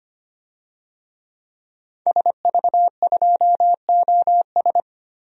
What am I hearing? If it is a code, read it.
Morse code, heard as a keyed tone, about 25 wpm, 700 Hz. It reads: SV2OS
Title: DXCC-100 Call Signs - 5 Characters - 25wpm